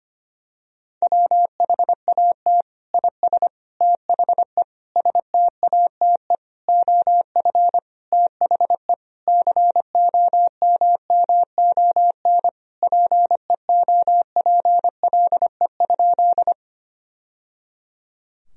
Morse code, heard as a keyed tone, about 25 wpm, 700 Hz. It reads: WHAT IS THE STATE OF THE COMMON PEOPLE?
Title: Sentences from Top 1000 Words - Review Part 3 - 25wpm